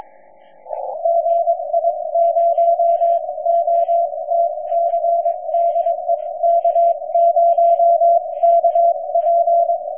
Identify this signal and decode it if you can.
Morse code, heard as a keyed tone, about 22 wpm, spaced as for 19 wpm, 680 Hz. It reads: GI0OTC DE K0KP
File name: 6M audio / k0kp otc